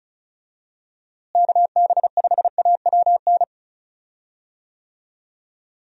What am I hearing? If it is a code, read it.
Morse code, heard as a keyed tone, about 34 wpm, 700 Hz. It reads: KB5AWN
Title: USA Call Signs - 6 Characters - Repeated 3x - 35wpm